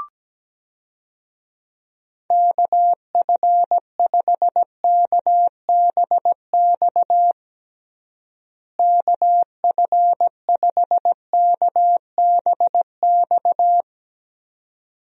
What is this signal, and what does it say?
Morse code, heard as a keyed tone, about 17 wpm, 700 Hz. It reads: KF5KBX KF5KBX
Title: USA Call Signs - 6 Characters - Repeated 2x - 17wpm